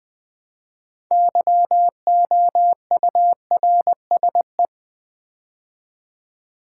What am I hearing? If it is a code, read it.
Morse code, heard as a keyed tone, about 21 wpm, 700 Hz. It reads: YOURSE
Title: Build A Word - 3 Letters At A Time - 8-Letter Words - 20wpm